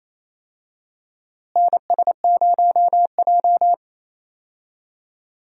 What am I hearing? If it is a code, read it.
Morse code, heard as a keyed tone, about 28 wpm, 700 Hz. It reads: NS0J
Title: USA Call Signs - 4 Characters - Repeated 2x - 28wpm